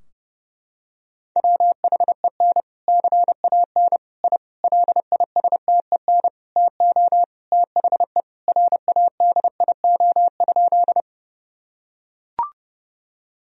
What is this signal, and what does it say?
Morse code, heard as a keyed tone, about 30 wpm, 700 Hz. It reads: WHEN CAN I LISTEN TO THE RADIO?
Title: Sentences from Top 1000 Words - Review Part 3 - 30wpm